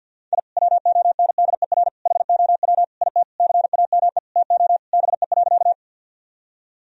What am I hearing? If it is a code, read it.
Morse code, heard as a keyed tone, about 50 wpm, 700 Hz. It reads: I WONDER HOW IT CAME TO BE.
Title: Sentences from Top 1000 Words - Review Part 3 - 50wpm